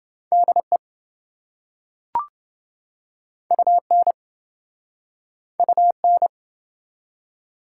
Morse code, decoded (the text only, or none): DE UN UN